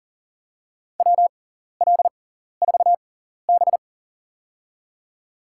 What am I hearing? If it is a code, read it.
Morse code, heard as a keyed tone, about 40 wpm, spaced as for 14 wpm, 700 Hz. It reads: WL4B
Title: USA Call Signs - 6x Character Spacing - 40wpm